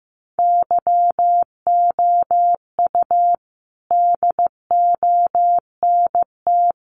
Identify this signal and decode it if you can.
Morse code, heard as a keyed tone, about 15 wpm, 700 Hz. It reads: YOU DONT
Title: Sentences from Top 1000 Words 15wpm